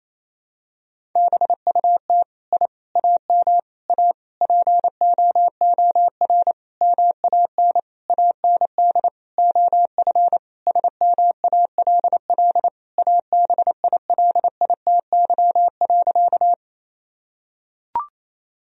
Morse code, heard as a keyed tone, about 28 wpm, 700 Hz. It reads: BUT I AM A POOR MAN AND OF SMALL ABILITY.
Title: Sentences from Top 1000 Words - Review Part 3 - 28wpm